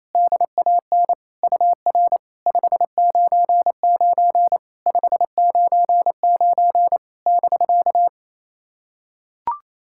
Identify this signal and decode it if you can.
Morse code, heard as a keyed tone, about 28 wpm, 690 Hz. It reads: DAN UR 599 599 <BK>
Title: SOTA QSO Practice 28wpm